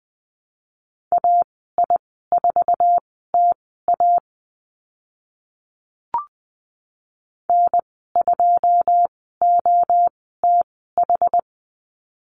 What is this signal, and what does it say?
Morse code, heard as a keyed tone, about 20 wpm, spaced as for 14 wpm, 700 Hz. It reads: AI4TA N2OTH
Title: 082 Lesson 14g - Number 2 - Call Signs 20wpm